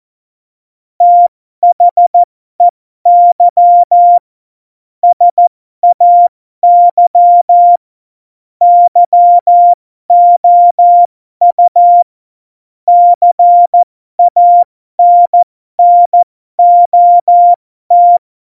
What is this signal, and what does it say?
Morse code, heard as a keyed tone, about 14 wpm, spaced as for 12 wpm, 700 Hz. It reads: THEY SAY YOU CANNOT
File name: Morsekurs / Witze